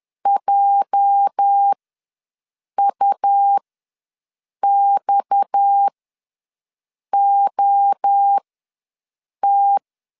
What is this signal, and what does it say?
Morse code, heard as a keyed tone, about 10 wpm, spaced as for 6 wpm, 800 Hz. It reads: JUXOT
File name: LEZIONI / 8 YJ45 / 8 YJ ws 10wpm 4043